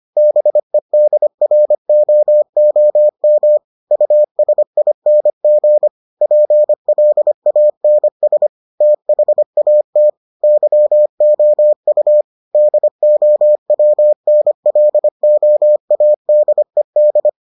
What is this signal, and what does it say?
Morse code, heard as a keyed tone, about 25 wpm, 600 Hz. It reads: BEDROOM USING PLANS THAT YOU DOWNLOADED